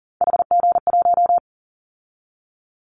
Morse code, heard as a keyed tone, about 41 wpm, 700 Hz. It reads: HG1
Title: Callsign Prefixes for Top 50 DXCC - 40wpm